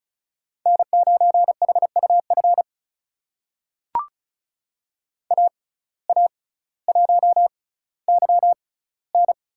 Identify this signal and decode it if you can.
Morse code, heard as a keyed tone, about 36 wpm, spaced as for 18 wpm, 700 Hz. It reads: N9HUF AA1YN
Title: USA Call Signs - 5 Characters - 6x Character Spacing - 35wpm